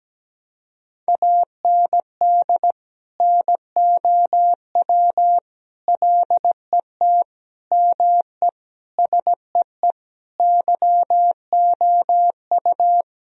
Morse code, heard as a keyed tone, about 17 wpm, 700 Hz. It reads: AND NOW LET ME SEE YOU
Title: Sentences from Top 1000 Words - Review Part 2 - 17wpm